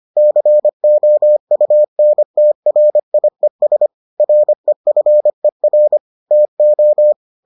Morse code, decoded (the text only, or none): COUNTRIES REFER TO